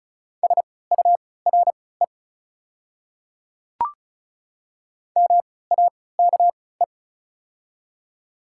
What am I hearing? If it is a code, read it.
Morse code, heard as a keyed tone, about 34 wpm, spaced as for 20 wpm, 700 Hz. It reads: SURE MAKE
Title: 3x Character Spacing - Top 300 Words - 35wpm